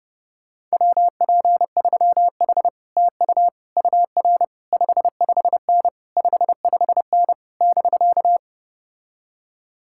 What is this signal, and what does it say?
Morse code, heard as a keyed tone, about 33 wpm, 700 Hz. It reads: WP3H TU UR 55N 55N <BK>